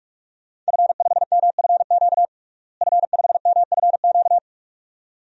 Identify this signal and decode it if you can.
Morse code, heard as a keyed tone, about 46 wpm, 700 Hz. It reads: F5MFQ F5MFQ
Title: DXCC-100 Call Signs - 5 Characters - Repeated 2x - 45wpm